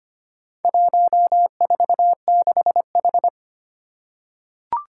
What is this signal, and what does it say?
Morse code, heard as a keyed tone, about 25 wpm, 700 Hz. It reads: 146H
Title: Four Challenging Letter-Number Combinations 25wpm